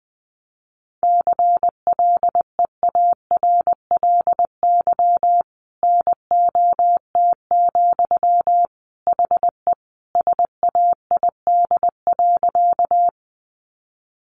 Morse code, heard as a keyed tone, about 20 wpm, 700 Hz. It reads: CLEARLY NOT, HE SAID.
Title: Sentences from Top 1300 Words 20wpm